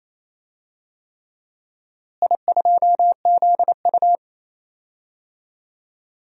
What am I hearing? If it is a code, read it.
Morse code, heard as a keyed tone, about 28 wpm, 700 Hz. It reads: I2ZU
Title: DXCC-100 Call Signs - 4 Characters - Repeated 3x - 28wpm